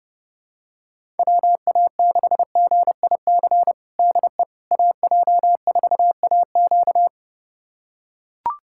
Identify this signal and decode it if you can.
Morse code, heard as a keyed tone, about 32 wpm, 700 Hz. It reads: WA6GIC DE AJ4AQ
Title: USA Callsign DE Callsign 30wpm